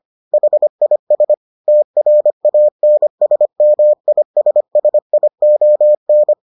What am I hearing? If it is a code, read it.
Morse code, heard as a keyed tone, about 25 wpm, 600 Hz. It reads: HIS TRANSMISSION